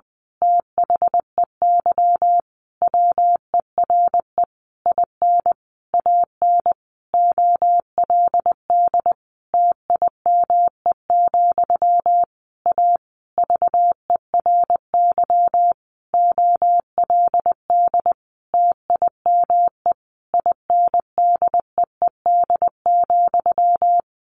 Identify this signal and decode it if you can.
Morse code, heard as a keyed tone, about 20 wpm, 700 Hz. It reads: THEY WERE IN AN OLD TIME, A VERY OLD TIME INDEED,
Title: A Sound Of Thunder 20wpm